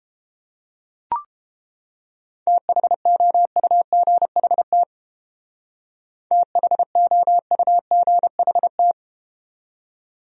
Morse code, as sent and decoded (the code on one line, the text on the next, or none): - .... --- ..- --. .... - / - .... --- ..- --. .... -
THOUGHT THOUGHT